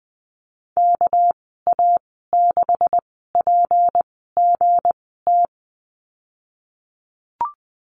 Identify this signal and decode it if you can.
Morse code, heard as a keyed tone, about 20 wpm, spaced as for 15 wpm, 700 Hz. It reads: KA6PGT
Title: 207 Lesson 33g - Letter j - Call Signs 20wpm